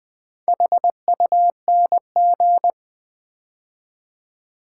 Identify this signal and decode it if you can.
Morse code, heard as a keyed tone, about 20 wpm, 700 Hz. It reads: HUNG
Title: Build A Word - 2 Letters At A Time - 6-Letter Words - 20wpm